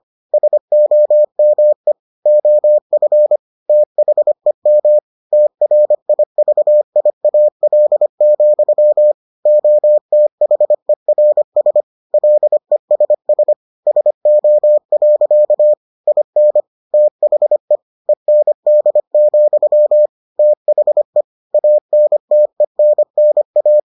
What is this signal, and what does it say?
Morse code, heard as a keyed tone, about 25 wpm, 600 Hz. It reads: SOME OF THEM TRIVIAL, OTHERS LESS SO. IN THE END, THE ANTENNA